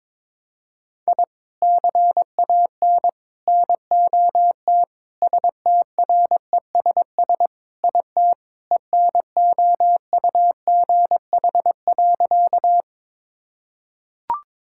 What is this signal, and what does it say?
Morse code, heard as a keyed tone, about 21 wpm, 700 Hz. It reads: I CAN NOT STRESS IT ENOUGH.
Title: Sentences from Top 1000 Words - Review Part 6 - 22wpm